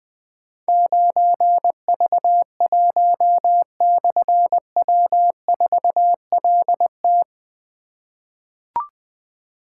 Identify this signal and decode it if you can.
Morse code, heard as a keyed tone, about 20 wpm, 700 Hz. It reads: 9V1/W4LT
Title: Difficult Call Signs 20wpm